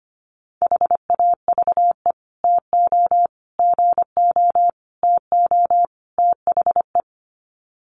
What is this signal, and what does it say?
Morse code, heard as a keyed tone, about 25 wpm, 700 Hz. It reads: HAVE TO GO TO THE
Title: Sets of 5 Words Encore 25wpm